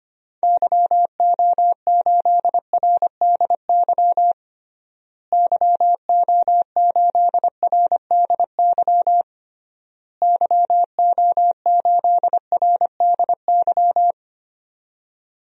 Morse code, heard as a keyed tone, about 25 wpm, 700 Hz. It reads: YO8RDY YO8RDY YO8RDY
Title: DXCC-100 Call Signs - 6 Characters - Repeated 3x - 25wpm